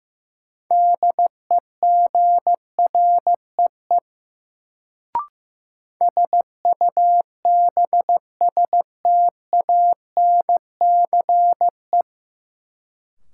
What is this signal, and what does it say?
Morse code, heard as a keyed tone, about 15 wpm, 700 Hz. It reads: DEGREE SUBSTANCE
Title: Sentences from Top 1500 Words - New Word Drill - 15wpm